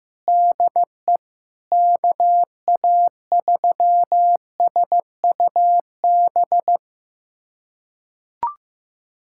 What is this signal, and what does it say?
Morse code, heard as a keyed tone, about 16 wpm, 700 Hz. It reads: DE KA3SUB
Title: USA Callsign DE Callsign - 6 Characters - 15wpm